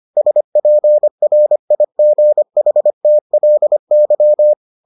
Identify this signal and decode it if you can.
Morse code, heard as a keyed tone, about 25 wpm, 600 Hz. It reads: SPRIGHTLY